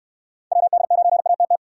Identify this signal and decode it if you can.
Morse code, heard as a keyed tone, about 69 wpm, 700 Hz. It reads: WD9ATT